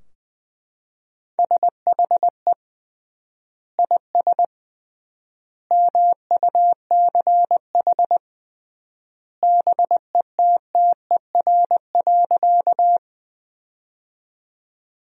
Morse code, heard as a keyed tone, about 20 wpm, 700 Hz. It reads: SHE IS MUCH BETTER.
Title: 3x Word-Spacing - Sentences from Top 400 Words - 20wpm